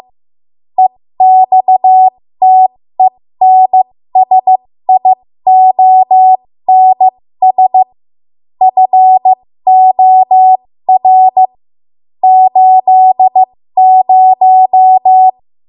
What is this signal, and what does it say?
Morse code, heard as a keyed tone, about 15 wpm, spaced as for 13 wpm, 750 Hz. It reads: EXTENSIONS FOR 80
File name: Morse / 13WPM